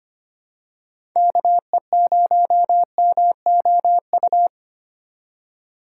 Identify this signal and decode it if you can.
Morse code, heard as a keyed tone, about 25 wpm, 700 Hz. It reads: KE0MOU
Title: USA Call Signs - 2x Character Spacing - 25wpm